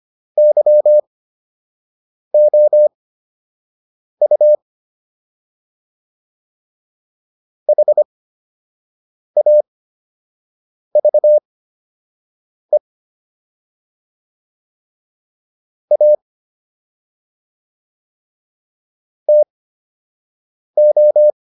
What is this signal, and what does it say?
Morse code, heard as a keyed tone, about 25 wpm, spaced as for 6 wpm, 600 Hz. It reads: YOU HAVE A TO